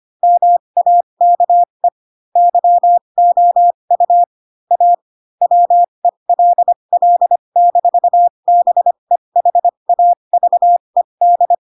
Morse code, heard as a keyed tone, about 25 wpm, spaced as for 22 wpm, 700 Hz. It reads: MAKE YOU A WELL-BEHAVED
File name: Pinocchio / Pinocchio ch3 18 25 ik1hge